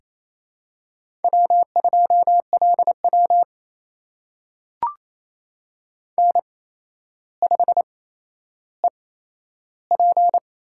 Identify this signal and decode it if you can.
Morse code, heard as a keyed tone, about 27 wpm, spaced as for 12 wpm, 700 Hz. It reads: W2LW N5EP
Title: USA Call Signs - 4 Characters - 8x Character Spacing - 28wpm